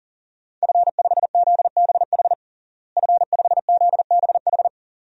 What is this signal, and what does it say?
Morse code, heard as a keyed tone, about 37 wpm, 700 Hz. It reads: F5ZBH F5ZBH
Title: DXCC-100 Call Signs - 5 Characters - Repeated 2x - 40wpm